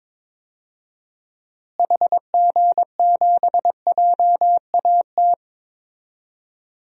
Morse code, heard as a keyed tone, about 21 wpm, 700 Hz. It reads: HG7JAT
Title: DXCC-100 Call Signs - 6 Characters - Repeated 2x - 22wpm